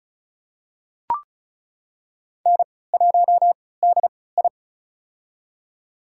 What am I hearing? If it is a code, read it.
Morse code, heard as a keyed tone, about 36 wpm, spaced as for 20 wpm, 700 Hz. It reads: N1DI
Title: USA Call Signs - 4 Characters - 3x Character Spacing - 35wpm